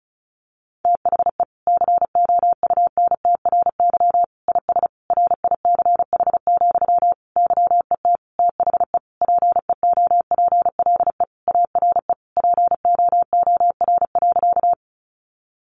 Morse code, heard as a keyed tone, about 34 wpm, 700 Hz. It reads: THE COUNTRY IS RICH, YET THE PEOPLE ARE POOR.
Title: Sentences from Top 1100 Words 35wpm